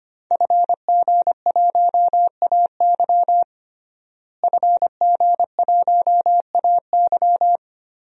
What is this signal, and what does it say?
Morse code, heard as a keyed tone, about 25 wpm, 700 Hz. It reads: FG1AY FG1AY
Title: DXCC-100 Call Signs - 5 Characters - Repeated 3x - 25wpm